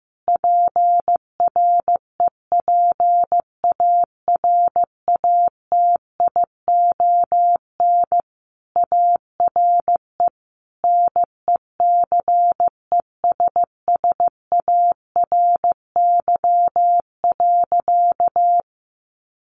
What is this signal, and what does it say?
Morse code, heard as a keyed tone, about 15 wpm, 700 Hz. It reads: PREPARATION ARE NECESSARY.
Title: Sentences from Top 1400 Words 15wpm